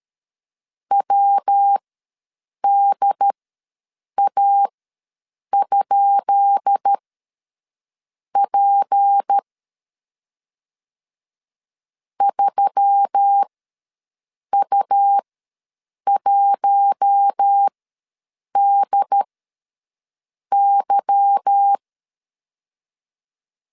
Morse code, heard as a keyed tone, about 13 wpm, spaced as for 6 wpm, 800 Hz. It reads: WDA?P 3U1DY
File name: LEZIONI / 10 SIGNS / 10B INT ws 12wpm 3527